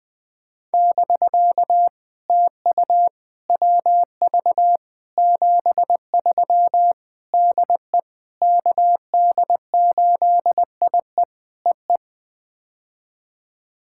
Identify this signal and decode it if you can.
Morse code, heard as a keyed tone, about 21 wpm, 700 Hz. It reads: <BK> TU WV 73 DE KD8IE EE
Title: POTA QSO Practice - 20wpm